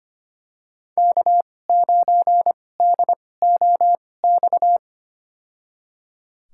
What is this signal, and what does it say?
Morse code, heard as a keyed tone, about 25 wpm, spaced as for 18 wpm, 700 Hz. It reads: K9DOX
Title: USA Call Signs - 2x Character Spacing - 25wpm